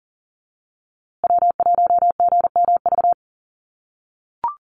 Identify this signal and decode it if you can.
Morse code, heard as a keyed tone, about 40 wpm, 700 Hz. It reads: W1GMV
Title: USA Call Signs Repeated 3x 40wpm